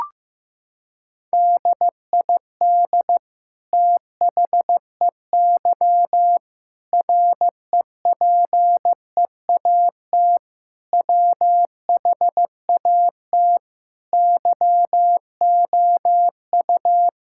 Morse code, heard as a keyed tone, about 15 wpm, 700 Hz. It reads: DID THEY REPEAT WHAT YOU
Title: Sentences from Top 1600 Words 15wpm